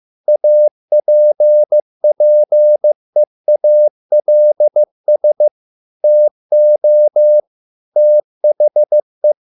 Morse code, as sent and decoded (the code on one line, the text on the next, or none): .- .--. .--. .- .-.. ... / - --- / - .... .
APPEALS TO THE